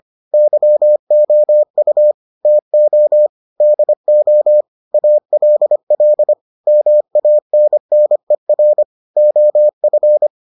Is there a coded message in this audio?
Yes: Morse code from 0.3 to 10.3 s, heard as a keyed tone, about 25 wpm, 600 Hz. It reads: YOU TO DO ALL MANNER OF